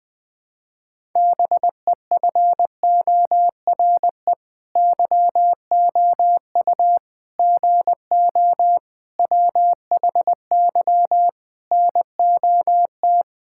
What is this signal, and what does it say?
Morse code, heard as a keyed tone, about 20 wpm, 700 Hz. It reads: BEFORE YOU GO WHY NOT
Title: Sets of 9 Words - 20wpm